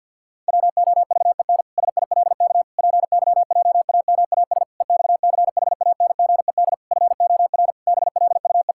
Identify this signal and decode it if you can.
Morse code, heard as a keyed tone, about 51 wpm, 700 Hz. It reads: WOVEN SILK PYJAMAS EXCHANGED FOR BLUE
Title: Pangram Challenge v2 - 50wpm